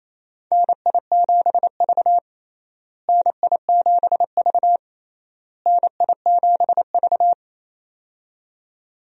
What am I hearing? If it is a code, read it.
Morse code, heard as a keyed tone, about 27 wpm, 700 Hz. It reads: NI7V NI7V NI7V
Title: USA Call Signs - 4 Characters - Repeated 3x - 28wpm